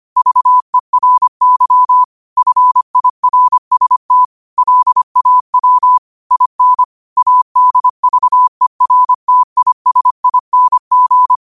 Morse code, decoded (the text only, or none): VERY FIRST LAW IN ADVERTISING